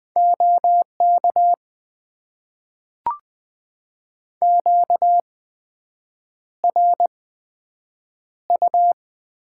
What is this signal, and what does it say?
Morse code, heard as a keyed tone, about 20 wpm, spaced as for 7 wpm, 700 Hz. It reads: OK QRU
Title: Top 90 QSO Elements - 8x Character Spacing - 20wpm